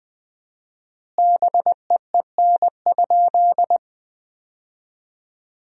Morse code, heard as a keyed tone, about 21 wpm, 700 Hz. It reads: BEEN?
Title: 4x Word-Spacing - Sentences from Top 200 Words - 20wpm